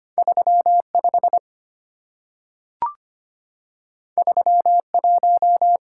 Morse code, heard as a keyed tone, about 25 wpm, 690 Hz. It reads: 35 31
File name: Two Letter-Number Combinations - 25wpm